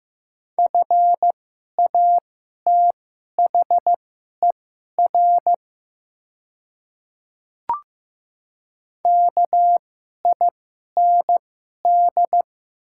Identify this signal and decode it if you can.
Morse code, heard as a keyed tone, about 15 wpm, spaced as for 11 wpm, 700 Hz. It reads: FATHER KIND
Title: 2x Character Spacing - Top 500 Words - 15wpm